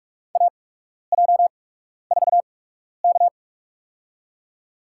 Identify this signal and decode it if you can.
Morse code, heard as a keyed tone, about 42 wpm, spaced as for 12 wpm, 700 Hz. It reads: AJ4K
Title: USA Call Signs - 8x Character Spacing - 45wpm